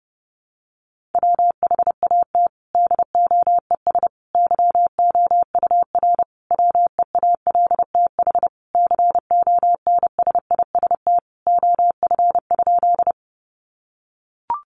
Morse code, heard as a keyed tone, about 30 wpm, 700 Hz. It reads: WHAT DOES YOUR WEALTH CONSIST OF?
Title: Sentences from Top 1400 Words 30wpm